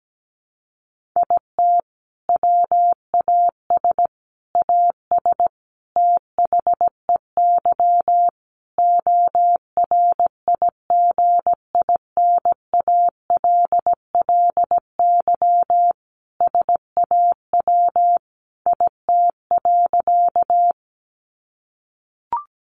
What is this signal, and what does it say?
Morse code, heard as a keyed tone, about 17 wpm, 700 Hz. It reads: IT WAS AS THEY ORIGINALLY SAW IT.
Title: Sentences from Top 1500 Words 17wpm